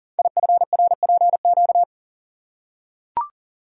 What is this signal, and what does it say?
Morse code, heard as a keyed tone, about 38 wpm, 700 Hz. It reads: IFRPQ
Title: Code Groups - 5 Letter:Numbers - 40wpm